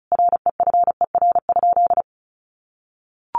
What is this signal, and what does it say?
Morse code, heard as a keyed tone, about 34 wpm, 700 Hz. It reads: REFER?
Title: Sentences from Top 1000 Words 35wpm